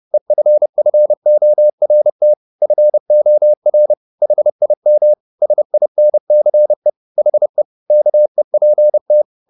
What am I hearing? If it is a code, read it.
Morse code, heard as a keyed tone, about 30 wpm, 600 Hz. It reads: EFFORT FOR HIM SINCE HE KEPT
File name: the old man and the sea / the old man and the sea 1 30wpm